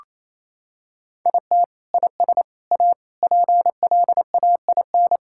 Morse code, heard as a keyed tone, about 28 wpm, 700 Hz. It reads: IT IS A PLAIN
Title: Sentences from Top 1000 Words - Review Part 5 - 28wpm